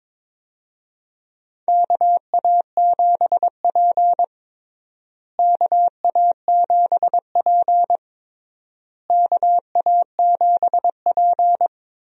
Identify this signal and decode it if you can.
Morse code, heard as a keyed tone, about 22 wpm, 700 Hz. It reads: KA7P KA7P KA7P
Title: USA Call Signs - 4 Characters - Repeated 3x - 22wpm